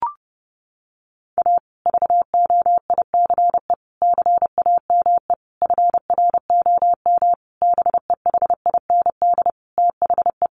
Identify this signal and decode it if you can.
Morse code, heard as a keyed tone, about 28 wpm, 700 Hz. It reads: A VOICE CAME FROM BEHIND THE
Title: Sentences from Top 1400 Words 30wpm